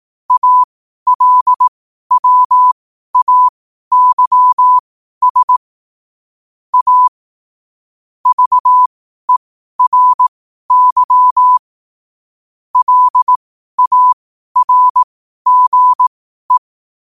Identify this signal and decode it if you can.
Morse code, heard as a keyed tone, about 18 wpm, spaced as for 12 wpm, 1000 Hz. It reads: ALWAYS A VERY LARGE
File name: qotdfiles / 12wpm